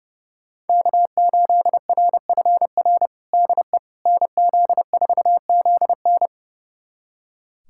Morse code, heard as a keyed tone, about 30 wpm, 700 Hz. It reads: K8RFR DE NZ4ZN
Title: USA Callsign DE Callsign - 5 Characters - 30wpm